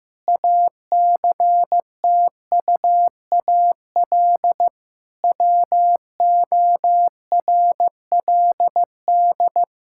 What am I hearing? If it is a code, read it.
Morse code, heard as a keyed tone, about 16 wpm, 700 Hz. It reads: ACTUAL WORLD
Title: Sentences from Top 1000 Words - Review Part 5 - 15wpm